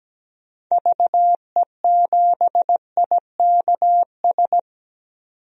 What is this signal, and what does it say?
Morse code, heard as a keyed tone, about 17 wpm, 700 Hz. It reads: VE7IKS
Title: DXCC-100 Call Signs - 6 Characters - 17wpm